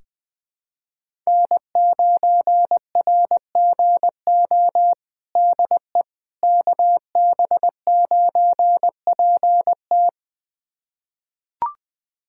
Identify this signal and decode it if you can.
Morse code, heard as a keyed tone, about 20 wpm, 700 Hz. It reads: N9RGO DE KB9PT
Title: USA Callsign DE Callsign - 5 Characters - 20wpm